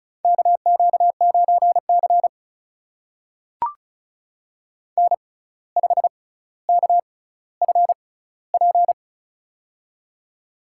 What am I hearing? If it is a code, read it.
Morse code, heard as a keyed tone, about 36 wpm, spaced as for 17 wpm, 700 Hz. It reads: KQ9C N5KFP
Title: USA Call Signs - 6x Character Spacing - 35wpm